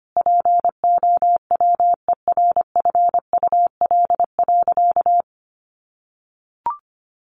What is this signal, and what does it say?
Morse code, heard as a keyed tone, about 25 wpm, 700 Hz. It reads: POWERFUL.
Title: Sentences from Top 1300 Words 25wpm